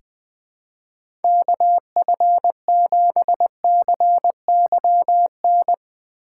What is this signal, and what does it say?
Morse code, heard as a keyed tone, about 20 wpm, 700 Hz. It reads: KF7CYN
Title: USA Call Signs - 6 Characters - 4x Character Spacing - 20wpm